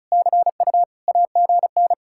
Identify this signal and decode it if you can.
Morse code, heard as a keyed tone, about 44 wpm, 700 Hz. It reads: CU AGN